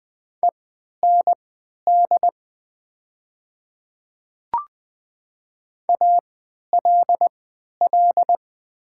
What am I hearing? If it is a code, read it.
Morse code, heard as a keyed tone, about 20 wpm, spaced as for 11 wpm, 700 Hz. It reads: END ALL
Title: 3x Character Spacing - Top 200 Words - 20wpm